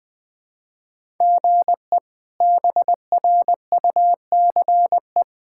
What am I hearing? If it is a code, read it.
Morse code, heard as a keyed tone, about 20 wpm, 700 Hz. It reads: GE BRUCE
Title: K1USN SST QSO Practice - 20wpm